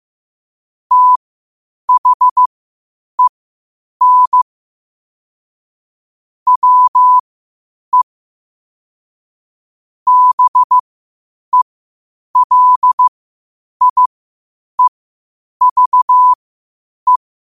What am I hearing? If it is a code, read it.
Morse code, heard as a keyed tone, about 15 wpm, spaced as for 8 wpm, 1000 Hz. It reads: THEN WE BELIEVE
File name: qotdfiles / 8wpm